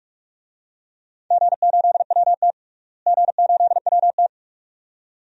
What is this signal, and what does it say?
Morse code, heard as a keyed tone, about 45 wpm, 700 Hz. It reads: G8WT G8WT
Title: DXCC-100 Call Signs - 4 Characters - Repeated 2x - 45wpm